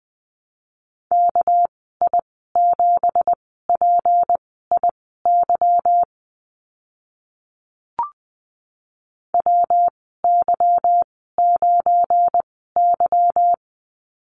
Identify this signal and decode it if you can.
Morse code, heard as a keyed tone, about 20 wpm, spaced as for 14 wpm, 700 Hz. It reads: KI7PIY WY9Y
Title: 213 Lesson 34g - Number 8 - Call Signs 20wpm